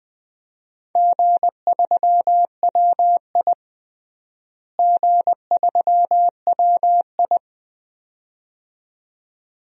Morse code, heard as a keyed tone, about 20 wpm, 700 Hz. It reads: G3WI G3WI
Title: DXCC-100 Call Signs - 4 Characters - Repeated 2x - 20wpm